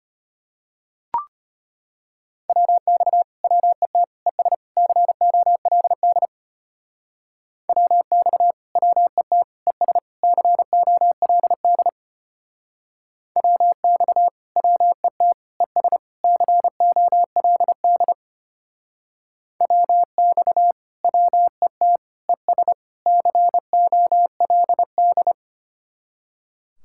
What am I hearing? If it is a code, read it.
Morse code, heard as a keyed tone, about 33 wpm, 700 Hz. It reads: WX WET ES COLD WX WET ES COLD WX WET ES COLD WX WET ES COLD